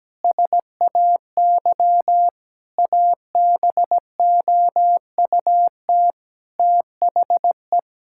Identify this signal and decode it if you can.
Morse code, heard as a keyed tone, about 17 wpm, 700 Hz. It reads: SAY ABOUT THE
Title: Sentences from Top 1000 Words - Review Part 5 - 17wpm